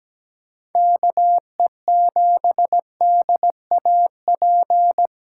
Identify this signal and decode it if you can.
Morse code, heard as a keyed tone, about 17 wpm, 700 Hz. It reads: KE7DAP
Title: USA Call Signs - 6 Characters - 4x Character Spacing - 17wpm